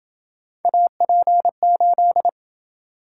Morse code, heard as a keyed tone, about 27 wpm, 700 Hz. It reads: AP8